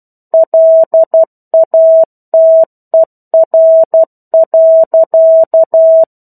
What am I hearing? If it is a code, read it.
Morse code, heard as a keyed tone, about 12 wpm, 650 Hz. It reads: LATER.
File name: 01-Chapter-01 12 WPM